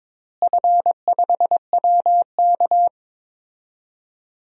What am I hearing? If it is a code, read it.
Morse code, heard as a keyed tone, about 22 wpm, 700 Hz. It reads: F5WK
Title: DXCC-100 Call Signs - 4 Characters - Repeated 2x - 22wpm